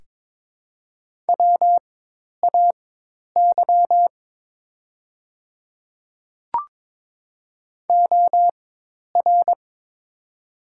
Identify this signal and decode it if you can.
Morse code, heard as a keyed tone, about 23 wpm, spaced as for 10 wpm, 700 Hz. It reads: WAY OR
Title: 4x Character Spacing - Top 100 Words - 22wpm